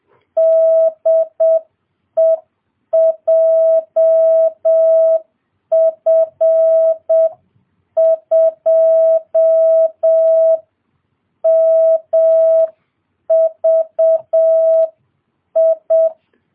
Morse code, heard as a keyed tone, about 7 wpm, 650 Hz. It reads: DE JF2MVI